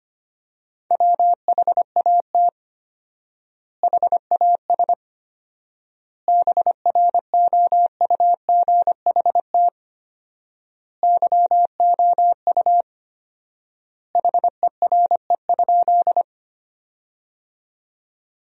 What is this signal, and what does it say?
Morse code, heard as a keyed tone, about 25 wpm, 700 Hz. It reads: WHAT HAS BROUGHT YOU HERE?
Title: 4x Word-Spacing - Sentences from Top 500 Words - 25wpm